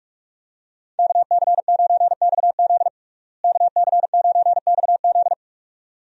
Morse code, heard as a keyed tone, about 46 wpm, 700 Hz. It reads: KC9XZ KC9XZ
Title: USA Call Signs - 5 Characters - Repeated 2x - 45wpm